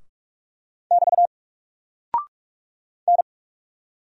Morse code, heard as a keyed tone, about 44 wpm, 700 Hz. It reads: <BT> N
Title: Single Character (Letter:Number:Punctuation:Prosign) 45wpm